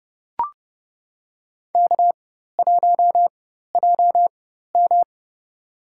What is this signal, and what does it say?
Morse code, heard as a keyed tone, about 30 wpm, spaced as for 14 wpm, 700 Hz. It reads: K1JM
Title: USA Call Signs - 4 Characters - 4x Character Spacing - 30wpm